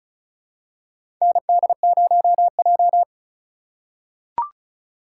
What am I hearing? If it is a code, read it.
Morse code, heard as a keyed tone, about 34 wpm, 700 Hz. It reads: ND0J
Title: USA Call Signs - 4 Characters - 2x Character Spacing - 35wpm